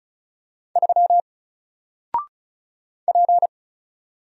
Morse code, heard as a keyed tone, about 34 wpm, 700 Hz. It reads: 3 P
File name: Single-Letter-Number-35wpm